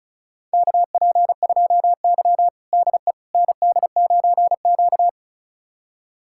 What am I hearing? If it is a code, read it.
Morse code, heard as a keyed tone, about 34 wpm, 700 Hz. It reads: KP2Y DE ND9Q
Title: USA Callsign DE Callsign - 4 Characters - 35wpm